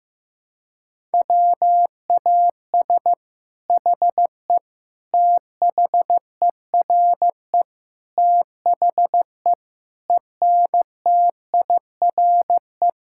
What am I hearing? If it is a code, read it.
Morse code, heard as a keyed tone, about 15 wpm, 700 Hz. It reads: WAS HE THERE THE ENTIRE
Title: Sentences from Top 1000 Words - Review Part 5 - 15wpm